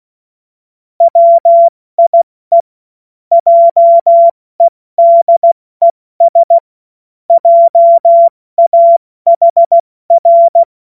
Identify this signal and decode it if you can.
Morse code, heard as a keyed tone, about 16 wpm, 680 Hz. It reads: WIE JEDES JAHR
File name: Morsekurs / Witze / Witz 25. April 70 F80